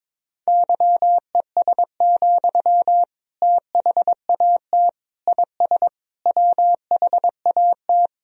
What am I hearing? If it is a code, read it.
Morse code, heard as a keyed tone, about 22 wpm, 700 Hz. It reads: YES, THAT IS WHAT
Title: Sentences from Top 500 Words and Abbreviations 22wpm